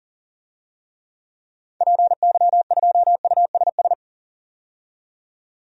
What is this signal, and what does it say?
Morse code, heard as a keyed tone, about 40 wpm, 700 Hz. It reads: PY2USS
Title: DXCC-100 Call Signs - 6 Characters - 40wpm